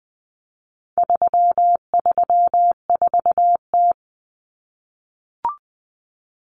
Morse code, heard as a keyed tone, about 20 wpm, 700 Hz. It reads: 334T
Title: Common Cut Numbers - 4-Digit 20wpm